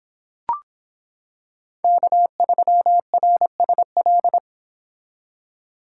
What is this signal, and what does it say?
Morse code, heard as a keyed tone, about 25 wpm, 700 Hz. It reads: K3RSL